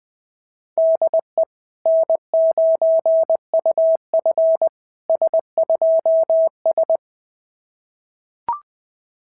Morse code, heard as a keyed tone, about 20 wpm, 690 Hz. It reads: DE N9UF S2S
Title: SOTA QSO Practice 20wpm